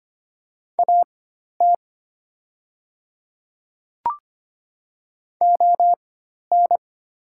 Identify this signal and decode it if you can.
Morse code, heard as a keyed tone, about 25 wpm, spaced as for 12 wpm, 700 Hz. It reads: AT ON
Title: 4x Character Spacing - Top 200 Words - 25wpm